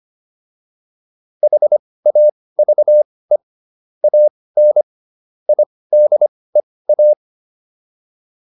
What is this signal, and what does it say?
Morse code, heard as a keyed tone, about 25 wpm, spaced as for 18 wpm, 600 Hz. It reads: HAVE AN IDEA
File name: PR305 18